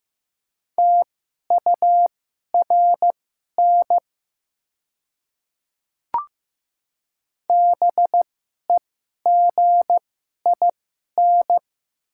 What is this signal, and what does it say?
Morse code, heard as a keyed tone, about 15 wpm, spaced as for 11 wpm, 700 Hz. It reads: TURN BEGIN
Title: 2x Character Spacing - Top 300 Words - 15wpm